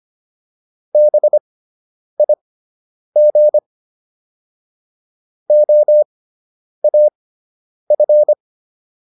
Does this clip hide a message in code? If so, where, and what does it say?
Yes: Morse code from 0.9 to 8.3 s, heard as a keyed tone, about 25 wpm, spaced as for 9 wpm, 600 Hz. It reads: BIG OAF